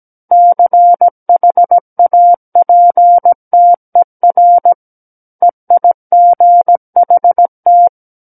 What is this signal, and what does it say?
Morse code, heard as a keyed tone, about 17 wpm, 700 Hz. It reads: CHAPTER EIGHT
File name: Book One / 08-Chapter 8 - 17 WPM